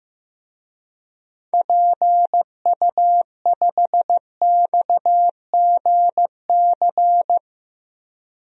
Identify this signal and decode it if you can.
Morse code, heard as a keyed tone, about 15 wpm, 700 Hz. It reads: PU5XGC
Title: DXCC-100 Call Signs - 6 Characters - 15wpm